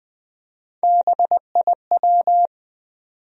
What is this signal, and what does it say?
Morse code, heard as a keyed tone, about 20 wpm, 700 Hz. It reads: BIW
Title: USA Call Sign Postfixes - 3 Letters 20wpm